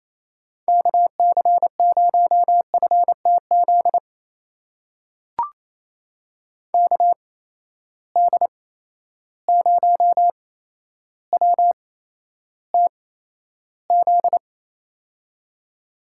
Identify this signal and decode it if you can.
Morse code, heard as a keyed tone, about 28 wpm, spaced as for 12 wpm, 700 Hz. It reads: KC0FTZ KD0WTZ
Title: USA Call Signs - 6 Characters - 8x Character Spacing - 28wpm